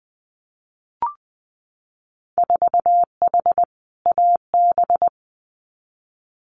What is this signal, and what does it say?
Morse code, heard as a keyed tone, about 20 wpm, 700 Hz. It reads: 4H AB
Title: Winter Field Day Exchanges Encore 20wpm